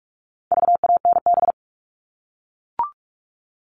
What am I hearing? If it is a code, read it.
Morse code, heard as a keyed tone, about 44 wpm, 700 Hz. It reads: VANB
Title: All Cut Numbers - 4-Digit 45wpm